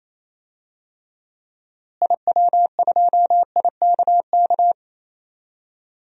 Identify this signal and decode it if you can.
Morse code, heard as a keyed tone, about 27 wpm, 700 Hz. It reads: IW2IKK
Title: DXCC-100 Call Signs - 6 Characters - Repeated 2x - 28wpm